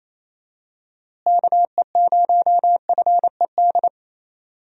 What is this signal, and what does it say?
Morse code, heard as a keyed tone, about 27 wpm, 700 Hz. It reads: KE0FED
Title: USA Call Signs - 6 Characters - Repeated 3x - 28wpm